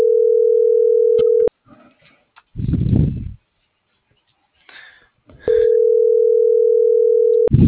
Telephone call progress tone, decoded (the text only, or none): ringback tone